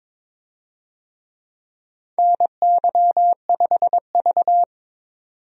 Morse code, heard as a keyed tone, about 22 wpm, 700 Hz. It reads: NY5V
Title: USA Call Signs - 4 Characters - 22wpm